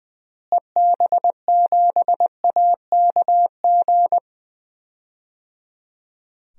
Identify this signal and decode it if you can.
Morse code, heard as a keyed tone, about 20 wpm, 700 Hz. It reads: EB7AKG
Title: DXCC-100 Call Signs - 6 Characters - Repeated 3x - 20wpm